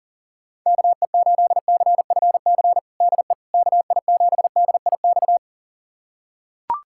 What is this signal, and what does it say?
Morse code, heard as a keyed tone, about 40 wpm, 700 Hz. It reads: KE8CFC DE KI7DIX
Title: USA Callsign DE Callsign - 6 Characters - 40wpm